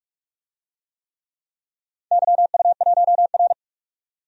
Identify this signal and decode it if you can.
Morse code, heard as a keyed tone, about 42 wpm, 700 Hz. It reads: YU1R
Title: DXCC-100 Call Signs - 4 Characters - 45wpm